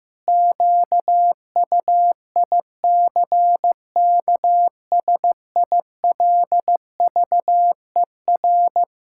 Morse code, heard as a keyed tone, about 15 wpm, 700 Hz. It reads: QUICKSILVER
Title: Pangram Challenge v2 - 15wpm